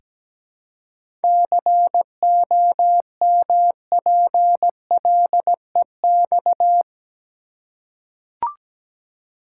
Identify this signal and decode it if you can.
Morse code, heard as a keyed tone, about 17 wpm, 700 Hz. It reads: COMPLEX
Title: Seven Letter Words - 17wpm